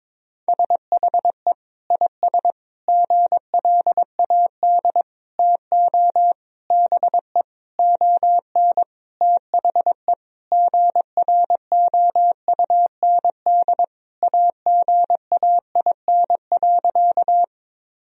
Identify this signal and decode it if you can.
Morse code, heard as a keyed tone, about 22 wpm, 700 Hz. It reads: SHE IS GLAD TO BE ON THE GROUND AGAIN.
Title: Sentences from Top 1000 Words - Review Part 2 - 22wpm